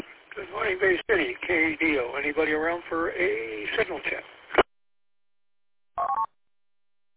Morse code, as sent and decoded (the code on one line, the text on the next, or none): none